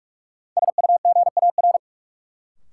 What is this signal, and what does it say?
Morse code, heard as a keyed tone, about 46 wpm, 700 Hz. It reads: SUGAR
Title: Sentences from Top 1200 Words - New Word Drill - 45wpm